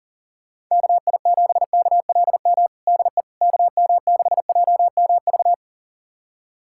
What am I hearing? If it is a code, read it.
Morse code, heard as a keyed tone, about 37 wpm, 690 Hz. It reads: KI7KLM DE KM6JMV
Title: USA Callsign DE Callsign - 6 Characters - 40wpm